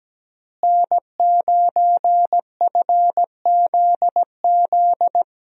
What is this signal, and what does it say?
Morse code, heard as a keyed tone, about 17 wpm, 700 Hz. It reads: N9FZZ
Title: USA Call Signs - 5 Characters - Repeated 3x - 17wpm